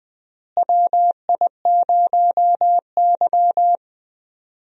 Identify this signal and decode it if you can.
Morse code, heard as a keyed tone, about 20 wpm, 700 Hz. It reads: WI0Y
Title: USA Call Signs - 4 Characters - Repeated 2x - 20wpm